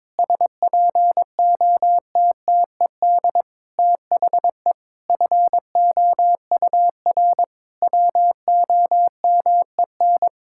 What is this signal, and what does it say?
Morse code, heard as a keyed tone, about 22 wpm, 700 Hz. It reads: SPOTTED THE FOUR WOMEN
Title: Pangram Challenge v2 - 22wpm